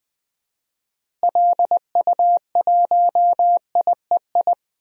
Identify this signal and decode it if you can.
Morse code, heard as a keyed tone, about 21 wpm, 700 Hz. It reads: LU1IEI
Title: DXCC-100 Call Signs - 6 Characters - Repeated 2x - 20wpm